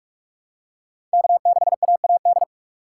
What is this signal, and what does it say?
Morse code, heard as a keyed tone, about 46 wpm, 700 Hz. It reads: K6AAD
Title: USA Call Signs - 5 Characters - 3x Character Spacing - 45wpm